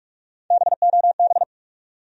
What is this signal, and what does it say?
Morse code, heard as a keyed tone, about 46 wpm, 700 Hz. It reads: BOB
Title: USA Common Names - Max 3 Letters - 45wpm